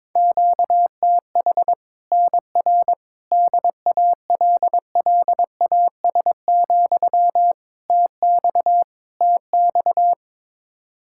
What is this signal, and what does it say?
Morse code, heard as a keyed tone, about 22 wpm, 700 Hz. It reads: QTH NR DALLAS, TX TX
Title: Practice QSO - Exchange 1 - Beginning - 22wpm